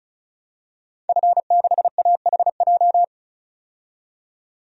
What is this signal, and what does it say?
Morse code, heard as a keyed tone, about 34 wpm, 700 Hz. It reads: F6AHJ
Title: DXCC-100 Call Signs - 5 Characters - 35wpm